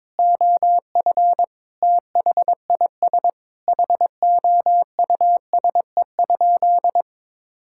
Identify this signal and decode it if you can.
Morse code, heard as a keyed tone, about 22 wpm, 700 Hz. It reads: OF THIS HOUSE?
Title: Sentences from Top 1600 Words 22wpm